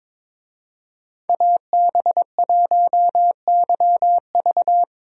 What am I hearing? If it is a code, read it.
Morse code, heard as a keyed tone, about 22 wpm, 700 Hz. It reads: AB1YV
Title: USA Call Signs - 5 Characters - Repeated 2x - 22wpm